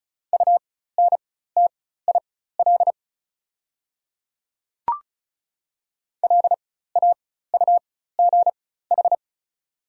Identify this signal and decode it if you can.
Morse code, heard as a keyed tone, about 34 wpm, spaced as for 16 wpm, 700 Hz. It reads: UNTIL LAUGH
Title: 4x Character Spacing - Top 500 Words - 35wpm